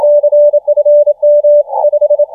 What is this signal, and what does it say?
Morse code, heard as a keyed tone, about 27 wpm, 590 Hz. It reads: CFM 5